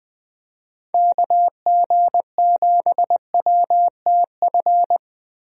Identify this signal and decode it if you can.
Morse code, heard as a keyed tone, about 20 wpm, 700 Hz. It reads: KG7WTF
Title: USA Call Signs - 6 Characters - Repeated 2x - 20wpm